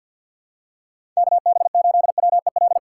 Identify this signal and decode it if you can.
Morse code, heard as a keyed tone, about 52 wpm, 700 Hz. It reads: KB8WEL
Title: USA Call Signs - 2x Character Spacing - 50wpm